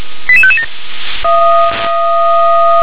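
Five-tone selective call (ZVEI), heard as quick runs of five tones, tones of about 80 ms. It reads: 88557
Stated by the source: ZVEI five-tone call 88796 received by SDR.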